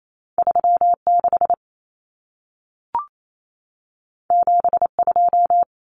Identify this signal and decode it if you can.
Morse code, heard as a keyed tone, about 28 wpm, 700 Hz. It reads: 36 72
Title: Common Cut Numbers - 2-Digit 28wpm